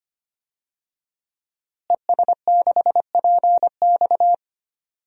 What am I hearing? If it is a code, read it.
Morse code, heard as a keyed tone, about 25 wpm, 700 Hz. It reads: ES6PX
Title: DXCC-100 Call Signs - 5 Characters - Repeated 3x - 25wpm